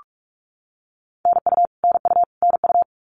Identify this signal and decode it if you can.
Morse code, heard as a keyed tone, about 44 wpm, 700 Hz. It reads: NU NU NU
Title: Canadian Provinces and Territories - Repeated 3x - 45wpm